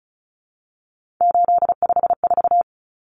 Morse code, heard as a keyed tone, about 34 wpm, 700 Hz. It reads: 854
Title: Common Cut Numbers - 3-Digit 35wpm